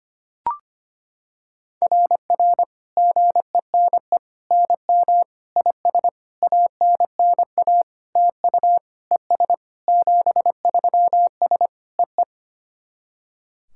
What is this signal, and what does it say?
Morse code, heard as a keyed tone, about 25 wpm, 690 Hz. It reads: RR GENE NM IS ANNA TU ES 73S EE
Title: SOTA QSO Practice 25wpm